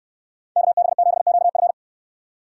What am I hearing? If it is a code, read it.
Morse code, heard as a keyed tone, about 69 wpm, 700 Hz. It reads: KB7QF